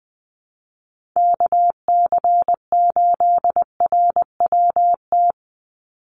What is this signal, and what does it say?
Morse code, heard as a keyed tone, about 19 wpm, 700 Hz. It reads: KC8RWT
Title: USA Call Signs 20wpm